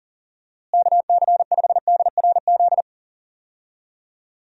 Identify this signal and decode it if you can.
Morse code, heard as a keyed tone, about 40 wpm, 700 Hz. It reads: KC5DRZ
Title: USA Call Signs - 6 Characters - Repeated 2x - 40wpm